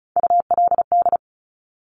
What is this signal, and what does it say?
Morse code, heard as a keyed tone, about 36 wpm, 700 Hz. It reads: ULD
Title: Top Three Letter Combinations in Words 35wpm